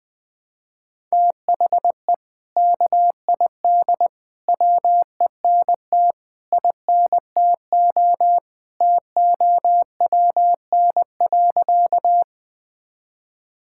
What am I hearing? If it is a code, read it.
Morse code, heard as a keyed tone, about 21 wpm, 700 Hz. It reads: THE KID WENT INTO TOWN.
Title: Sentences from Top 1600 Words 20wpm